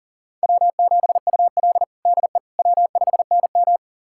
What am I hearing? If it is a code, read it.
Morse code, heard as a keyed tone, about 40 wpm, 700 Hz. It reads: W7UL DE W5NM
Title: USA Callsign DE Callsign - 4 Characters - 40wpm